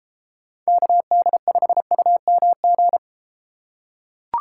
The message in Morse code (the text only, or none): KD5UMG